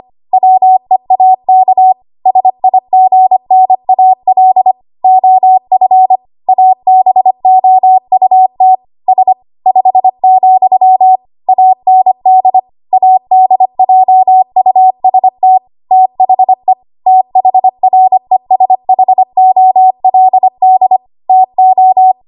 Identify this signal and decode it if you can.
Morse code, heard as a keyed tone, about 25 wpm, 750 Hz. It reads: WEAK SIGNAL OF ABOUT S 5, AND ADJUST THE THRESHOLD TO
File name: Archive / 25 WPM / 25WPM